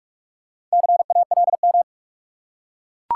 Morse code, heard as a keyed tone, about 43 wpm, 700 Hz. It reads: CALM
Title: Four Letter Words - 45wpm